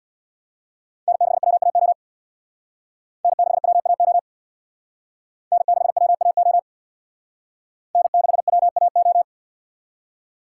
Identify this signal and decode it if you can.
Morse code, heard as a keyed tone, about 64 wpm, 700 Hz. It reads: N6WAO N6WAO N6WAO N6WAO